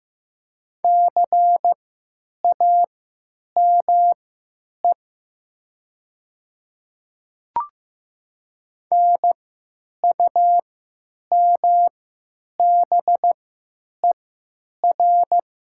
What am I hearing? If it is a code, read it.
Morse code, heard as a keyed tone, about 15 wpm, spaced as for 9 wpm, 700 Hz. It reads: CAME NUMBER
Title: 3x Character Spacing - Top 200 Words - 15wpm